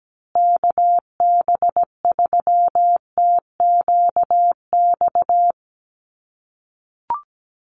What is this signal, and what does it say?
Morse code, heard as a keyed tone, about 17 wpm, 700 Hz. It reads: KB3TQX
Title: USA Call Signs Repeated 2x 17wpm